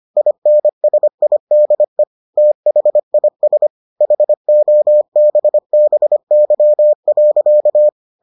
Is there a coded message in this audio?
Yes: Morse code from 0.2 to 7.9 s, heard as a keyed tone, about 25 wpm, 600 Hz. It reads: INSIDE THIS HOBBY.